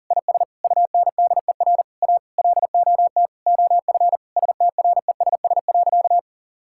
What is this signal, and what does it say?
Morse code, heard as a keyed tone, about 40 wpm, 700 Hz. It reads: IS UNDER A LOT OF STRESS.
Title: Sentences from Top 1000 Words - Review Part 6 - 40wpm